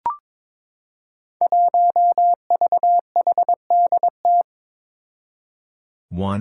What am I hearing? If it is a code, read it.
Morse code, heard as a keyed tone, about 22 wpm, 700 Hz. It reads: 1VHDT
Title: Five Challenging Letter-Number Combinations 22wpm